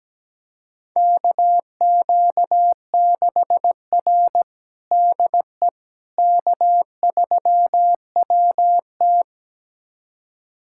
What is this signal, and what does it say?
Morse code, heard as a keyed tone, about 17 wpm, 700 Hz. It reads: KQ6R DE K3WT
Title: USA Callsign DE Callsign - 4 Characters - 17wpm